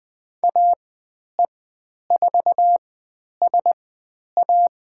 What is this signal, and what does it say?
Morse code, heard as a keyed tone, about 20 wpm, spaced as for 10 wpm, 700 Hz. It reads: AE4SA
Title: 109 Lesson 18g - Letter m - Call Signs 20wpm